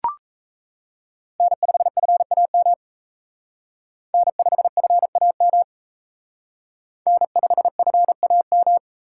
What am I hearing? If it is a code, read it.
Morse code, heard as a keyed tone, about 35 wpm, 700 Hz. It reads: N5FAM N5FAM N5FAM
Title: Speed Racing - USA 1x3 Call Signs 28wpm